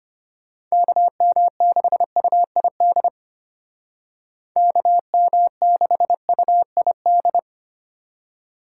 Morse code, heard as a keyed tone, about 30 wpm, 700 Hz. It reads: KM6UID KM6UID
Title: Speed Racing - USA 2x3 Call Signs 25wpm